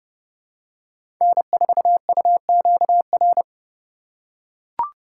Morse code, heard as a keyed tone, about 30 wpm, 700 Hz. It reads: N4UQR
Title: USA Call Signs - 4x Character Spacing - 30wpm